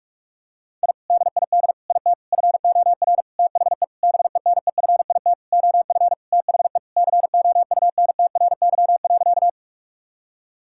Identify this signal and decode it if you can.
Morse code, heard as a keyed tone, about 46 wpm, 700 Hz. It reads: I DID IT FOR THE BENEFIT OF THE COUNTRY.
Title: Sentences from Top 1000 Words - Review Part 6 - 45wpm